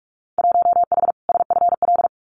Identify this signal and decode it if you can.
Morse code, heard as a keyed tone, about 44 wpm, 700 Hz. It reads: 1H SFL
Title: Winter Field Day Exchanges 45wpm